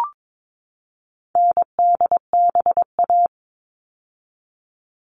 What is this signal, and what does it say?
Morse code, heard as a keyed tone, about 22 wpm, 700 Hz. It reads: NDBA
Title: All Cut Numbers - 4-Digit 22wpm